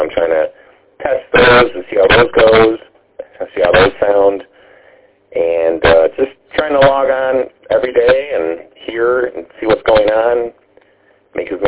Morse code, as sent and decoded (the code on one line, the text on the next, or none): none